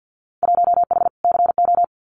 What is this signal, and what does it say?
Morse code, heard as a keyed tone, about 50 wpm, 700 Hz. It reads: 1H CO
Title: Winter Field Day Exchanges 50wpm